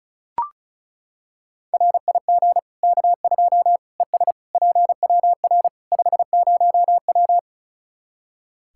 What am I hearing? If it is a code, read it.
Morse code, heard as a keyed tone, about 34 wpm, 700 Hz. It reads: RIG K2 ES PWR 50W
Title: Practice QSO - Exchange 2 - Info - 35wpm